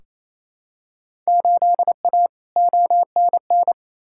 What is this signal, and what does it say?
Morse code, heard as a keyed tone, about 28 wpm, 700 Hz. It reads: 8A ONN
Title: Field Day Exchanges Encore - 28wpm